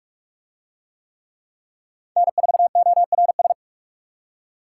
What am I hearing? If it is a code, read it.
Morse code, heard as a keyed tone, about 44 wpm, 700 Hz. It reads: N4ORS